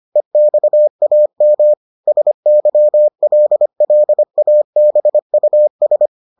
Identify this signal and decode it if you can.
Morse code, heard as a keyed tone, about 25 wpm, 600 Hz. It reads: EXAM SYLLABUS